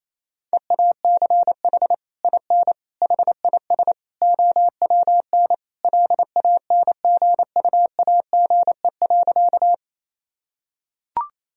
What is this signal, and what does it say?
Morse code, heard as a keyed tone, about 28 wpm, 700 Hz. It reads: EACH IN HIS OWN LANGUAGE.
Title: Sentences from Top 1000 Words - Review Part 3 - 28wpm